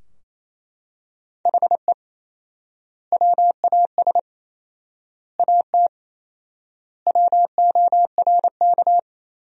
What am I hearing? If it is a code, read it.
Morse code, heard as a keyed tone, about 28 wpm, 700 Hz. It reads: HE WAS AT WORK
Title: 4x Word-Spacing - Sentences from Top 200 Words - 28wpm